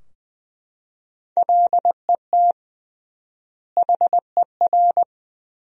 Morse code, heard as a keyed tone, about 20 wpm, 700 Hz. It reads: LET HER
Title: 3x Word-Spacing - Sentences from Top 500 Words - 20wpm